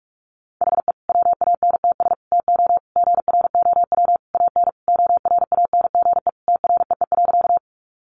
Morse code, heard as a keyed tone, about 44 wpm, 700 Hz. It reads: HE WANTS TO GROW AN ORANGE TREE.
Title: Easy Phrases 45wpm